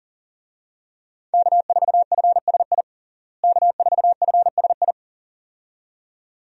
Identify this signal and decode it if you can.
Morse code, heard as a keyed tone, about 40 wpm, 700 Hz. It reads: K4FSI K4FSI
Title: USA Call Signs - 5 Characters - Repeated 2x - 40wpm